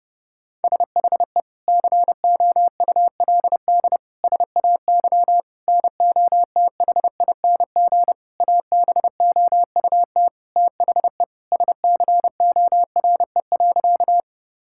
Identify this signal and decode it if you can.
Morse code, heard as a keyed tone, about 30 wpm, 700 Hz. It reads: SHE COULD SAY NOTHING ABOUT THE SCORE.
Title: Sentences from Top 1600 Words 30wpm